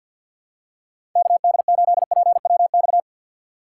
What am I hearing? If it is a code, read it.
Morse code, heard as a keyed tone, about 52 wpm, 700 Hz. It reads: KD8PWX
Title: USA Call Signs - 2x Character Spacing - 50wpm